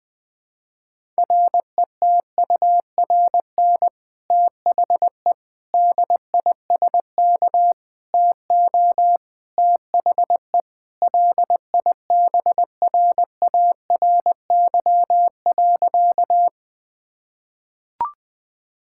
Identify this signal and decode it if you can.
Morse code, heard as a keyed tone, about 20 wpm, 700 Hz. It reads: RETURN THE DISK TO THE LIBRARY.
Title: Sentences from Top 1000 Words - Review Part 6 - 20wpm